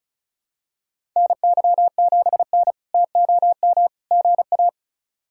Morse code, heard as a keyed tone, about 36 wpm, 700 Hz. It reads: NY7N TOM GA
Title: K1USN SST QSO Practice - 35wpm